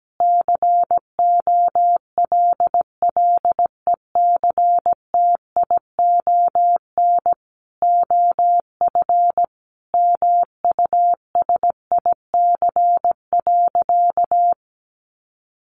Morse code, heard as a keyed tone, about 17 wpm, 700 Hz. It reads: COLLECTION OF MUSIC.
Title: Sentences from Top 1200 Words 17wpm